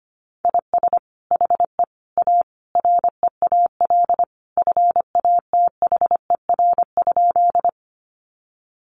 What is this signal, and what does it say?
Morse code, heard as a keyed tone, about 25 wpm, 700 Hz. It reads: IS HE A REAL FATHER?
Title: Sentences from Top 300 Words 25wpm